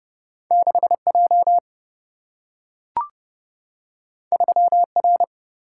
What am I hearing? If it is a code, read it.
Morse code, heard as a keyed tone, about 30 wpm, 700 Hz. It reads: 6 J 3 R
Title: Single-Letter Single-Number 30wpm